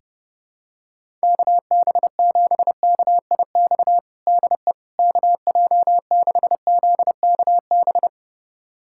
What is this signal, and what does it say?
Morse code, heard as a keyed tone, about 32 wpm, 700 Hz. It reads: KB7KIX DE KJ6ZKB
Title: USA Callsign DE Callsign 30wpm